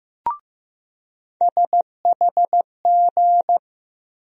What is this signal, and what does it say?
Morse code, heard as a keyed tone, about 16 wpm, 700 Hz. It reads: SHG
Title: USA Call Sign Postfixes - 3 Letters 15wpm